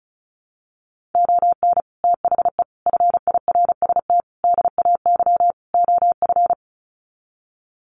Morse code, heard as a keyed tone, about 34 wpm, 700 Hz. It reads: ON THE FIRST DAY OF
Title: Sets of 5 Words Encore 35wpm